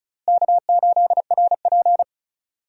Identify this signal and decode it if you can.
Morse code, heard as a keyed tone, about 34 wpm, 700 Hz. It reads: K8RP
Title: USA Call Signs - 4 Characters - Repeated 3x - 35wpm